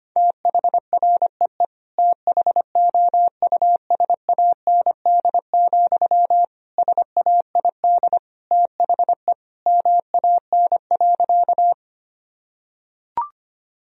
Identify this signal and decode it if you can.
Morse code, heard as a keyed tone, about 25 wpm, 700 Hz. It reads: THREE THOUSAND, SAID THE MAN.
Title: Sentences from Top 1000 Words - Review Part 3 - 25wpm